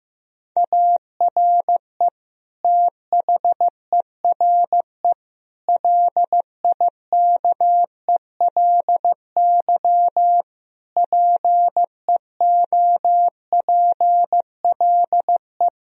Morse code, heard as a keyed tone, about 15 wpm, 700 Hz. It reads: ARE THERE LIKELY PEOPLE
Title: Sentences from Top 1000 Words - Review Part 4 - 15wpm